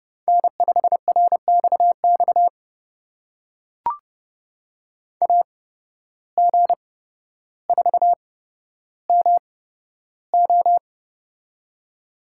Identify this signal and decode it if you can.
Morse code, heard as a keyed tone, about 30 wpm, spaced as for 13 wpm, 700 Hz. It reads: N5RXX AG4MO
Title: USA Call Signs - 5 Characters - 8x Character Spacing - 30wpm